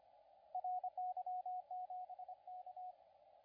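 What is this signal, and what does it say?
Morse code, heard as a keyed tone, about 26 wpm, 720 Hz, with radio noise. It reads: RY7K
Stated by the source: CW contest audio recorded at LA4C during the SAC CW contest.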